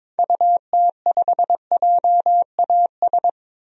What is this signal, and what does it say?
Morse code, heard as a keyed tone, about 22 wpm, 700 Hz. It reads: UT5JAS
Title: DXCC-100 Call Signs - 6 Characters - 22wpm